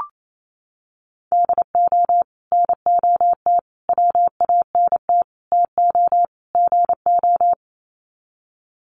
Morse code, heard as a keyed tone, about 28 wpm, 700 Hz. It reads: DO NOT WANT TO GO
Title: Sets of 5 Words 28wpm